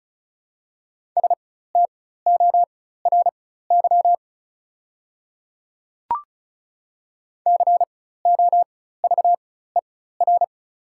Morse code, heard as a keyed tone, about 36 wpm, spaced as for 16 wpm, 690 Hz. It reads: STORY COVER
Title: 4x Character Spacing - Top 300 Words - 35wpm